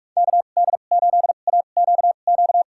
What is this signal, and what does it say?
Morse code, heard as a keyed tone, about 46 wpm, spaced as for 32 wpm, 700 Hz. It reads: KD8AQQ
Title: USA Call Signs - 6 Characters - 2x Character Spacing - 45wpm